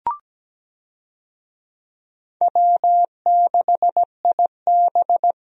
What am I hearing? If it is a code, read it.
Morse code, heard as a keyed tone, about 17 wpm, 700 Hz. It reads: W6IB
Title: USA Call Signs - 4 Characters - Repeated 3x - 17wpm